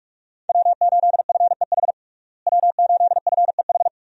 Morse code, heard as a keyed tone, about 41 wpm, 700 Hz. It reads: W8FEH W8FEH
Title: USA Call Signs - 5 Characters - Repeated 2x - 45wpm